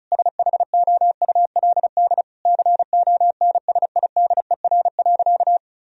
Morse code, heard as a keyed tone, about 34 wpm, 700 Hz. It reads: SHOULD CONSIDER.
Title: Sentences from Top 1000 Words - Review Part 4 - 35wpm